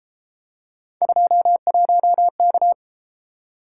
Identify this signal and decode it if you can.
Morse code, heard as a keyed tone, about 32 wpm, 700 Hz. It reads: 21K